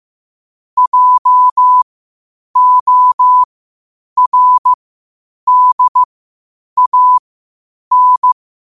Morse code, heard as a keyed tone, about 15 wpm, spaced as for 8 wpm, 1000 Hz. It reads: JORDAN
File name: qotdfiles / 8wpm